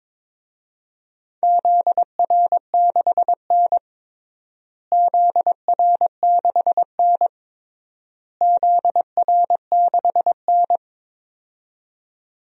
Morse code, heard as a keyed tone, about 21 wpm, 700 Hz. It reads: ZR6N ZR6N ZR6N
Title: DXCC-100 Call Signs - 4 Characters - Repeated 3x - 22wpm